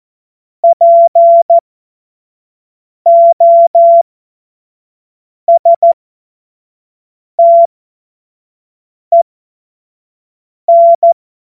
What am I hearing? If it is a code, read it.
Morse code, heard as a keyed tone, about 14 wpm, spaced as for 5 wpm, 680 Hz. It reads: POSTEN
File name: Morsekurs / Witze / Witz 10. Januar 2023 25 F70